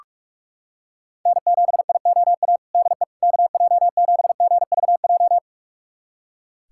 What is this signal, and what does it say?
Morse code, heard as a keyed tone, about 43 wpm, 700 Hz. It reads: N7IOA DE KJ7GVJ
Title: USA Callsign DE Callsign 45wpm